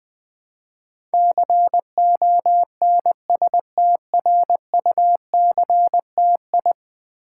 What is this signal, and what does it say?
Morse code, heard as a keyed tone, about 20 wpm, 700 Hz. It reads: CONSTRUCTI
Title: Build A Word - 5 Letters At A Time - 12-Letter Words - 20wpm